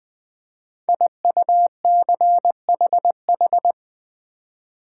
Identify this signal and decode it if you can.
Morse code, heard as a keyed tone, about 20 wpm, 700 Hz. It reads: IUCHH
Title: Code Groups - 5 Letters - 20wpm